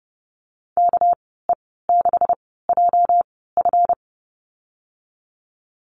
Morse code, heard as a keyed tone, about 30 wpm, spaced as for 17 wpm, 700 Hz. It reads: KE6JF
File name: Lesson 37g - Letter x - Call Signs 30wpm